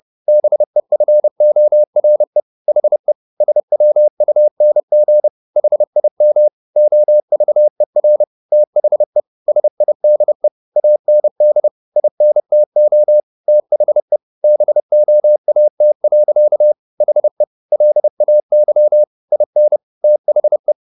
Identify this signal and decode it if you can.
Morse code, heard as a keyed tone, about 30 wpm, 600 Hz. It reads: BEFORE HE SWUNG HIM OVER THE SIDE AND INTO THE BOAT. HE LAY IN THE
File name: the old man and the sea / the old man and the sea 1 30wpm